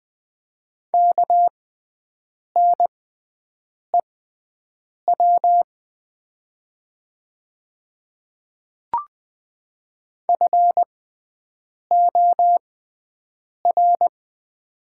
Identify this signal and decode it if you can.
Morse code, heard as a keyed tone, about 20 wpm, spaced as for 7 wpm, 700 Hz. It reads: KNEW FOR
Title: 6x Character Spacing - Top 500 Words - 20wpm